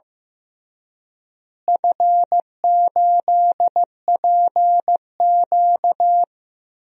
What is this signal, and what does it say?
Morse code, heard as a keyed tone, about 15 wpm, 700 Hz. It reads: F8PQ
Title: DXCC-100 Call Signs - 4 Characters - Repeated 3x - 15wpm